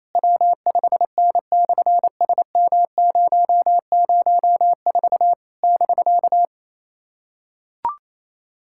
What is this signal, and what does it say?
Morse code, heard as a keyed tone, about 28 wpm, 700 Hz. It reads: W5N/SM004 <BK>
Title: SOTA QSO Practice 28wpm